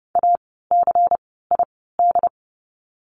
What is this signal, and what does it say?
Morse code, heard as a keyed tone, about 30 wpm, spaced as for 17 wpm, 700 Hz. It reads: ACID
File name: Lesson 16f - Letter c - Words 30wpm